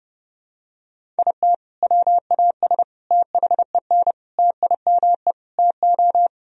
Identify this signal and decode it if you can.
Morse code, heard as a keyed tone, about 30 wpm, 700 Hz. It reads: IT WAS THEN TIME TO
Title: Sentences from Top 1000 Words - Review Part 1 - 30wpm